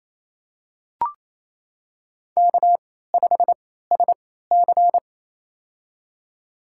Morse code, heard as a keyed tone, about 28 wpm, spaced as for 16 wpm, 700 Hz. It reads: K5SC
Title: USA Call Signs - 4 Characters - 3x Character Spacing - 28wpm